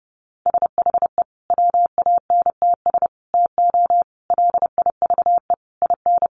Morse code, heard as a keyed tone, about 30 wpm, 700 Hz. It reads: SHE WANTS TO LIVE IN
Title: Easy Phrases 30wpm